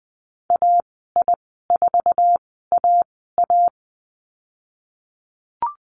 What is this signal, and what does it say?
Morse code, heard as a keyed tone, about 20 wpm, spaced as for 14 wpm, 700 Hz. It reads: AI4AA
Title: 048 Lesson 9g - Number 4 - Call Signs 20wpm